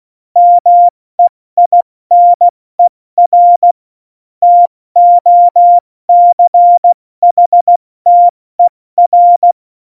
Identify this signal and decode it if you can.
Morse code, heard as a keyed tone, about 16 wpm, spaced as for 14 wpm, 700 Hz. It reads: MEINER TOCHTER
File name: Morsekurs / Witze / Witz 09. Juli 2019, 70 F80